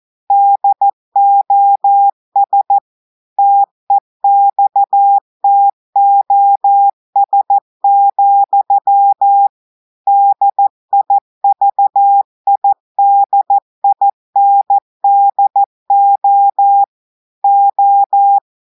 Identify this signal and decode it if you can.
Morse code, heard as a keyed tone, about 14 wpm, 800 Hz. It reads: DOS TEXTOS, DIVIDINDO O
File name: audios / portugais / portugais texte en clair 1 vitesse 14 mots minutes